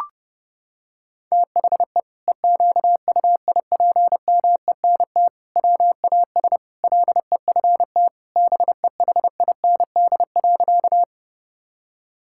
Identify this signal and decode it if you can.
Morse code, heard as a keyed tone, about 32 wpm, 700 Hz. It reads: THE EQUIPMENT WAS LEFT BEHIND.
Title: Sentences from Top 1000 Words - Review Part 5 - 30wpm